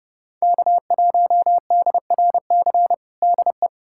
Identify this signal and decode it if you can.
Morse code, heard as a keyed tone, about 32 wpm, 700 Hz. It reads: K1DRC DE
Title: USA Callsign DE Callsign 30wpm